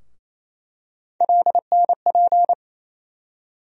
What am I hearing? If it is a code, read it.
Morse code, heard as a keyed tone, about 28 wpm, 700 Hz. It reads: LNP
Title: USA Call Sign Postfixes - 3 Letters 28wpm